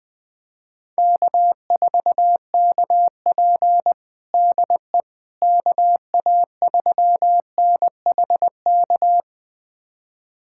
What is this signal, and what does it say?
Morse code, heard as a keyed tone, about 20 wpm, 700 Hz. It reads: K4KP DE KA3NHK
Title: USA Callsign DE Callsign 20wpm